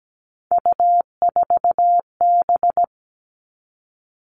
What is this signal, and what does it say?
Morse code, heard as a keyed tone, about 17 wpm, 700 Hz. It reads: U4B
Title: Three Challenging Letter-Number Combinations 17wpm